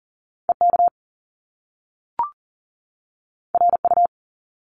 Morse code, heard as a keyed tone, about 40 wpm, 700 Hz. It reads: EK RU
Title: Top Two Letter Combinations in Words 40wpm